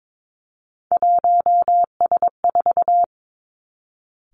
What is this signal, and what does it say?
Morse code, heard as a keyed tone, about 22 wpm, 700 Hz. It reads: 1S4
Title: Three Challenging Letter-Number Combinations 22wpm